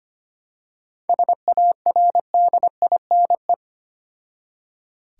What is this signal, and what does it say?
Morse code, heard as a keyed tone, about 25 wpm, 700 Hz. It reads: SARDINE
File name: practice / 25wpm / Difficult Words with Hints - 25wpm